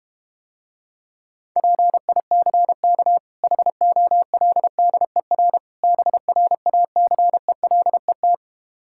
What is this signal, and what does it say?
Morse code, heard as a keyed tone, about 32 wpm, 700 Hz. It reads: PICK HOLDER BRACELET